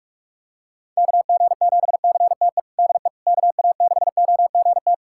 Morse code, heard as a keyed tone, about 46 wpm, 700 Hz. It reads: KG7CTE DE KA6OGT
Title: USA Callsign DE Callsign - 6 Characters - 45wpm